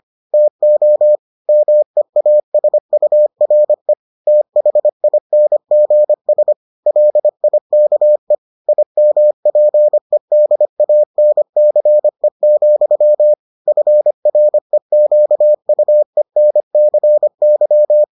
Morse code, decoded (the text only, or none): TO MEASURE THINGS LIKE IMPEDANCE, FREQUENCY